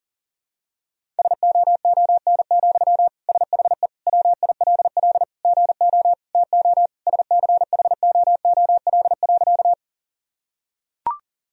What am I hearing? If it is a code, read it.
Morse code, heard as a keyed tone, about 40 wpm, 700 Hz. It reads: SOON, SHE WILL GO TO SCHOOL.
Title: Sentences from Top 1000 Words - Review Part 1 - 40wpm